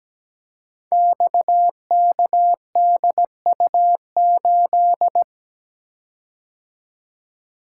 Morse code, heard as a keyed tone, about 17 wpm, 700 Hz. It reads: XKDU8
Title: Code Groups - 5 Letter:Numbers - 17wpm